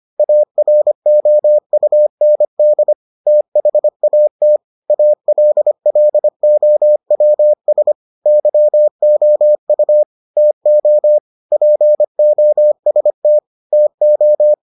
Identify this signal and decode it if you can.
Morse code, heard as a keyed tone, about 25 wpm, 600 Hz. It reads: AROUND THAT ALLOWS YOU TO POST TO